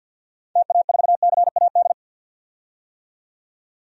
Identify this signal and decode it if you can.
Morse code, heard as a keyed tone, about 52 wpm, 700 Hz. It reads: TA4CAD
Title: DXCC-100 Call Signs - 6 Characters - 50wpm